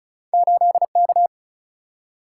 Morse code, heard as a keyed tone, about 36 wpm, 700 Hz. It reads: 8 K
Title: Single-Letter Single-Number 35wpm